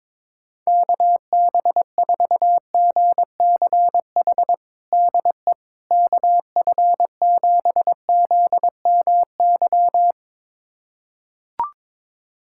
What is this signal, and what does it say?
Morse code, heard as a keyed tone, about 22 wpm, 700 Hz. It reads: KB4GCH DE KF7ZMY
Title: USA Callsign DE Callsign 22wpm